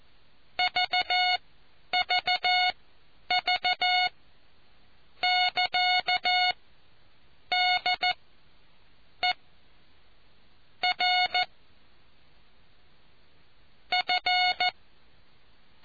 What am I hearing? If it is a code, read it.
Morse code, heard as a keyed tone, about 14 wpm, spaced as for 9 wpm, 730 Hz. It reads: VVV <KA> D E R F